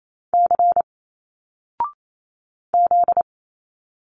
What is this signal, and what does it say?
Morse code, heard as a keyed tone, about 28 wpm, 700 Hz. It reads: C Z
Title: Single Letters 28wpm